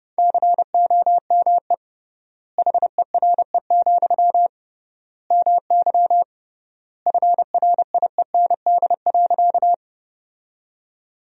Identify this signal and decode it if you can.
Morse code, heard as a keyed tone, about 29 wpm, 700 Hz. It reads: COME HERE, MY FRIEND.
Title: 3x Word-Spacing - Sentences from Top 300 Words - 30wpm